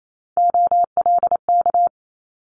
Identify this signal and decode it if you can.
Morse code, heard as a keyed tone, about 28 wpm, 700 Hz. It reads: OLK
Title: Multi-letter Phonemes Quick Fire 28wpm